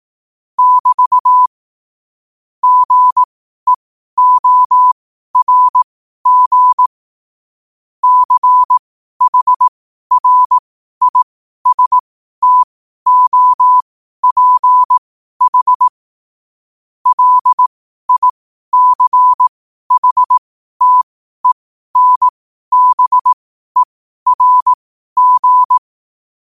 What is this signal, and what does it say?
Morse code, heard as a keyed tone, about 18 wpm, spaced as for 12 wpm, 1000 Hz. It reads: <BT> GEORG CHRISTOPH LICHTENBERG
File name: qotdfiles / 12wpm